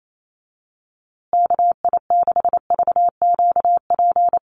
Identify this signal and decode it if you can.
Morse code, heard as a keyed tone, about 28 wpm, 700 Hz. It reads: KI6VQP
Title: USA Call Signs Repeated 3x 28wpm